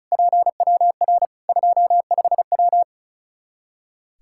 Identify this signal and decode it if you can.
Morse code, heard as a keyed tone, about 36 wpm, 700 Hz. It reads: PWR 25W
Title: Practice QSO - Exchange 2 - Info - 35wpm